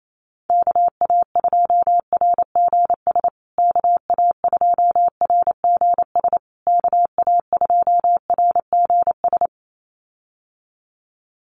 Morse code, heard as a keyed tone, about 28 wpm, 700 Hz. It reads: KA2RGS KA2RGS KA2RGS
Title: USA Call Signs Repeated 3x 28wpm